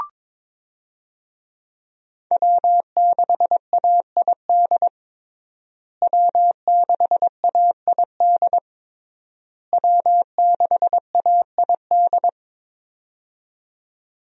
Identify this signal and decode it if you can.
Morse code, heard as a keyed tone, about 23 wpm, 700 Hz. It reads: W6AID W6AID W6AID
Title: USA Call Signs - 5 Characters - Repeated 3x - 22wpm